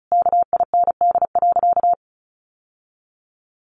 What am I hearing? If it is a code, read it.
Morse code, heard as a keyed tone, about 34 wpm, 700 Hz. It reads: KIND.
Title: Sentences from Top 1000 Words 35wpm